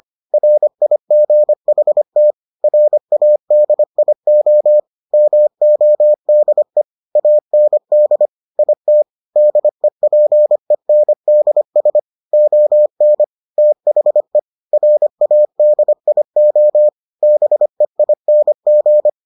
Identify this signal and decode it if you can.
Morse code, heard as a keyed tone, about 25 wpm, 600 Hz. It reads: RIGHT RADIO MODE AND IT DEPENDS ON THE RADIO BEING